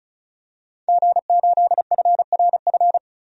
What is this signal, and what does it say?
Morse code, heard as a keyed tone, about 35 wpm, 700 Hz. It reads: G8FRF
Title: DXCC-100 Call Signs - 5 Characters - 35wpm